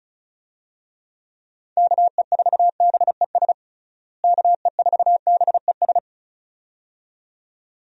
Morse code, heard as a keyed tone, about 34 wpm, 700 Hz. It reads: KE4BES KE4BES
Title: USA Call Signs - 6 Characters - Repeated 2x - 35wpm